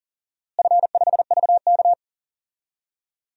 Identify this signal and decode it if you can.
Morse code, heard as a keyed tone, about 40 wpm, 700 Hz. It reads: F5VK
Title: DXCC-100 Call Signs - 4 Characters - Repeated 2x - 40wpm